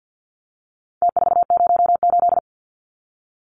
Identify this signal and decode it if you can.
Morse code, heard as a keyed tone, about 53 wpm, 700 Hz. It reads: T408
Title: Numbers - Contest Style - 4-Digit 50wpm